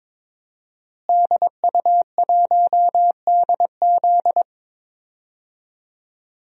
Morse code, heard as a keyed tone, about 22 wpm, 700 Hz. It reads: DU1DZ
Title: DXCC-100 Call Signs - 5 Characters - Repeated 3x - 22wpm